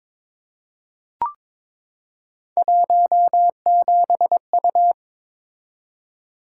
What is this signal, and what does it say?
Morse code, heard as a keyed tone, about 22 wpm, 700 Hz. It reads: 17U
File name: practice / 22wpm / Three Letter-Number Combinations - 22wpm